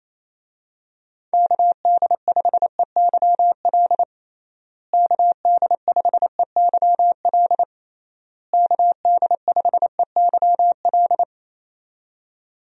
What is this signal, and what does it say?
Morse code, heard as a keyed tone, about 28 wpm, 700 Hz. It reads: KD5EYL KD5EYL KD5EYL
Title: USA Call Signs - 6 Characters - Repeated 3x - 28wpm